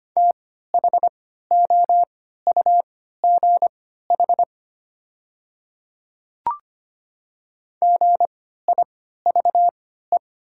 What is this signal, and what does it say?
Morse code, heard as a keyed tone, about 25 wpm, spaced as for 14 wpm, 700 Hz. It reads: THOUGH GIVE
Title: 3x Character Spacing - Top 500 Words - 25wpm